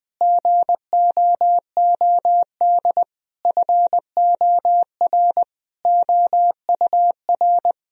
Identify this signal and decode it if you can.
Morse code, heard as a keyed tone, about 20 wpm, 700 Hz. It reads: GOOD FOR OUR
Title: Sentences from Top 1000 Words - Review Part 4 - 20wpm